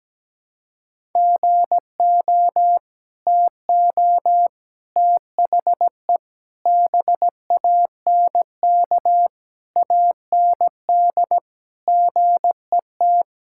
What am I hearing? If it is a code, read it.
Morse code, heard as a keyed tone, about 17 wpm, 700 Hz. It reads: GO TO THE BANK AND GET
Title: Sentences from Top 1000 Words - Review Part 5 - 17wpm